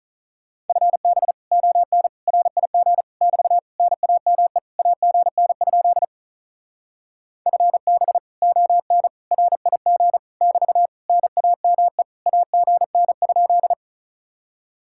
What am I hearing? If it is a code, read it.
Morse code, heard as a keyed tone, about 45 wpm, 700 Hz. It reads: FB ON RIG <BT> NAME AGN? FB ON RIG <BT> NAME AGN?